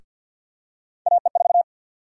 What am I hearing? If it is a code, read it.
Morse code, heard as a keyed tone, about 49 wpm, 700 Hz. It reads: AE4
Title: USA Call Sign Prefixes - 50wpm